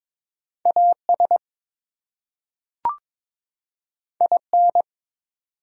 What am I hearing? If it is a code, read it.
Morse code, heard as a keyed tone, about 22 wpm, 700 Hz. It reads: AS IN
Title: Two Letter Words - 22wpm